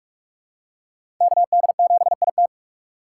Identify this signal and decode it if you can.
Morse code, heard as a keyed tone, about 45 wpm, 700 Hz. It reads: KD7IT
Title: USA Call Signs - 6x Character Spacing - 45wpm